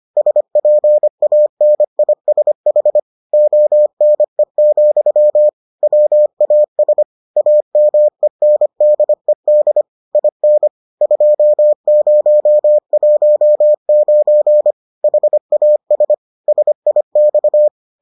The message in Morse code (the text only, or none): SPANISH ONE, WAS AMENDED IN 2019 HAS SIX